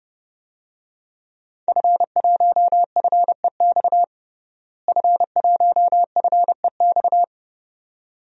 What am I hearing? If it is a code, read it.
Morse code, heard as a keyed tone, about 32 wpm, 700 Hz. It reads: F1FEX F1FEX
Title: DXCC-100 Call Signs - 5 Characters - Repeated 2x - 30wpm